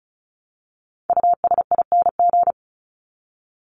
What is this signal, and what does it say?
Morse code, heard as a keyed tone, about 34 wpm, 700 Hz. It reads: USING
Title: ING Suffix - Top 100 Words with Base Words 35wpm